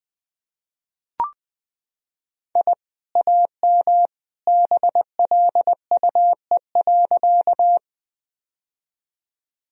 Morse code, heard as a keyed tone, about 20 wpm, 700 Hz. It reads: I AM BLUE.
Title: Sentences from Top 1000 Words - Review Part 3 - 20wpm